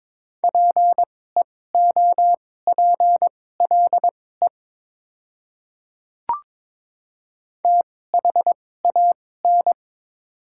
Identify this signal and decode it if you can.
Morse code, heard as a keyed tone, about 22 wpm, spaced as for 16 wpm, 690 Hz. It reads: PEOPLE THAN
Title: 2x Character Spacing - Top 100 Words - 22wpm